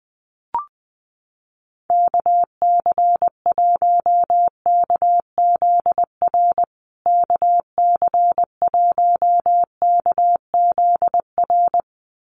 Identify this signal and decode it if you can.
Morse code, heard as a keyed tone, about 20 wpm, 700 Hz. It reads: KC1KZR KC1KZR
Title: USA Call Signs Repeated 2x 20wpm